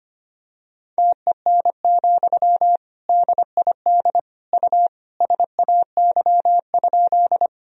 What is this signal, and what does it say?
Morse code, heard as a keyed tone, about 25 wpm, 700 Hz. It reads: TEN, DID U SAY?
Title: Sentences from Top 400 Words and Abbreviations 25wpm